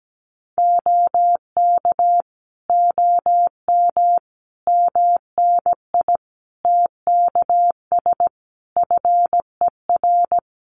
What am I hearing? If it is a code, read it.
Morse code, heard as a keyed tone, about 17 wpm, 700 Hz. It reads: OK OM MNI TKS FER
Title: Common QSO Phrases 17wpm